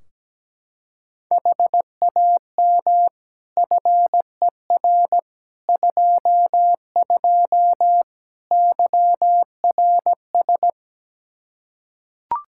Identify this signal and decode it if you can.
Morse code, heard as a keyed tone, about 17 wpm, 700 Hz. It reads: HAM FER 22 YRS
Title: K1USN SST QSO Practice 17wpm